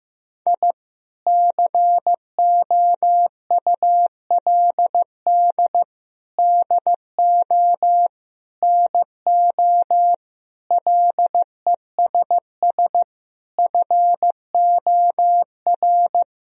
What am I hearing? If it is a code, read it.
Morse code, heard as a keyed tone, about 15 wpm, 700 Hz. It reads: I COULD DO NO LESS FOR
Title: Sentences from Top 1000 Words - Review Part 2 - 15wpm